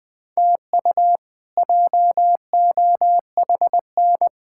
Morse code, heard as a keyed tone, about 20 wpm, 700 Hz. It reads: TU JOHN
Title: K1USN SST QSO Practice - 20wpm